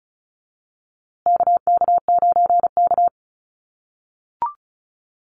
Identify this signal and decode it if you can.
Morse code, heard as a keyed tone, about 34 wpm, 700 Hz. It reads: KK9K
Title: USA Call Signs Repeated 3x 35wpm